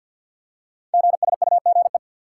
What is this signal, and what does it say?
Morse code, heard as a keyed tone, about 51 wpm, 700 Hz. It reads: GSUGE
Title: Code Groups - 5 Letters - 50wpm